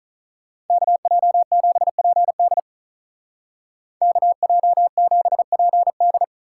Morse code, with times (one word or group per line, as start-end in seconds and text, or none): 0.7-2.6 KJ7PD
4.0-6.2 KJ7PD